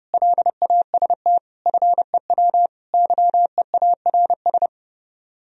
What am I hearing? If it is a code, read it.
Morse code, heard as a keyed tone, about 30 wpm, 700 Hz. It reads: LAST FEW YEARS